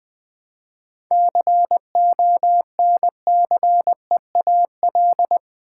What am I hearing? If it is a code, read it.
Morse code, heard as a keyed tone, about 20 wpm, 700 Hz. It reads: CONCEAL
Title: 7 Letter Words - First 3 Letters Given - 20wpm